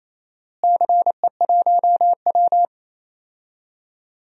Morse code, heard as a keyed tone, about 28 wpm, 700 Hz. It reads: CE1W
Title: DXCC-100 Call Signs - 4 Characters - Repeated 2x - 28wpm